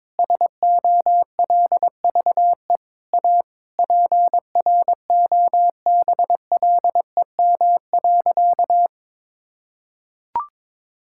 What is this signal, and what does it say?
Morse code, heard as a keyed tone, about 22 wpm, 700 Hz. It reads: SOLVE A PROBLEM.
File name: Sentences with 9-Word Phrases - 22wpm